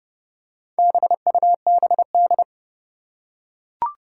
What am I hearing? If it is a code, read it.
Morse code, heard as a keyed tone, about 30 wpm, 700 Hz. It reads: BUBD
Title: Four Challenging Letter-Number Combinations 30wpm